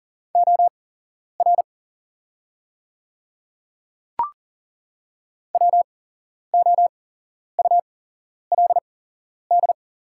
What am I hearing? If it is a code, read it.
Morse code, heard as a keyed tone, about 40 wpm, spaced as for 11 wpm, 690 Hz. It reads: OR WOULD
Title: 8x Character Spacing - Top 100 Words - 40wpm